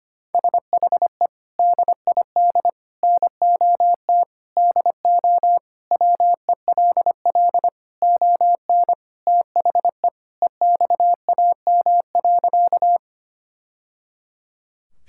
Morse code, heard as a keyed tone, about 25 wpm, 700 Hz. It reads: SHE DID NOT DO WELL ON THE EXAM.
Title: Sentences from Top 1000 Words - Review Part 5 - 25wpm